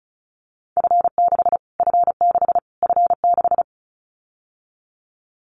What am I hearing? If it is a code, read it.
Morse code, heard as a keyed tone, about 34 wpm, 700 Hz. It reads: F6 F6 F6
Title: Callsign Prefixes for Top 25 DXCC - Repeated 3x - 35wpm